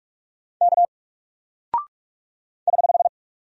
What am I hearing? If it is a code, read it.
Morse code, heard as a keyed tone, about 46 wpm, 700 Hz. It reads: K <HH>
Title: Single Character (Letter:Number:Punctuation:Prosign) Alt 45wpm